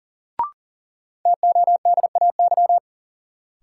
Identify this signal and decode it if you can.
Morse code, heard as a keyed tone, about 40 wpm, 700 Hz. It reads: TODAY
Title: Sets of 5 Words Encore - New Word Drill - 40wpm